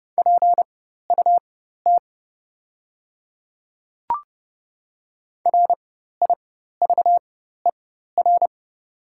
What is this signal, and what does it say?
Morse code, heard as a keyed tone, about 29 wpm, spaced as for 14 wpm, 700 Hz. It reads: PUT RIVER
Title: 4x Character Spacing - Top 300 Words - 30wpm